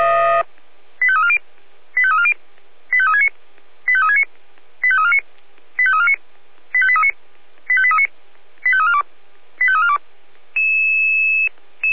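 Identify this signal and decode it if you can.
Five-tone selective call (ZVEI), heard as quick runs of five tones, tones of about 70 ms. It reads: 85340 85340 85369 85369 85349 85349 85839 85839 85342 85342
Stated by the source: ZVEI five-tone call received by SDR.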